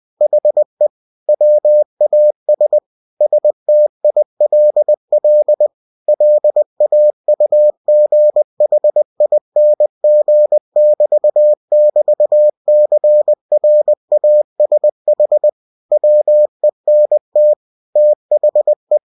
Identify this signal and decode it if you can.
Morse code, heard as a keyed tone, about 20 wpm, 600 Hz. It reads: HE WAS STILL LAUGHING<BT><BT>CRASH WENT THE
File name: cw-books / winnie-the-pooh / winnie-20wpm-page-0043